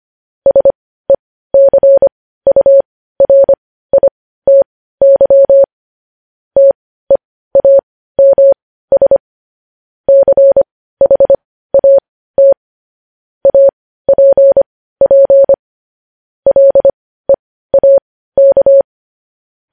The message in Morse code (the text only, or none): SECURITY TEAMS CHAT APP LEAK